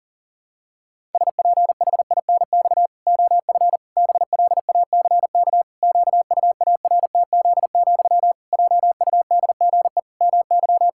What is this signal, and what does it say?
Morse code, heard as a keyed tone, about 37 wpm, 700 Hz. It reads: SPHINX OF BLACK QUARTZ, JUDGE MY
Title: Pangram Challenge v2 - 40wpm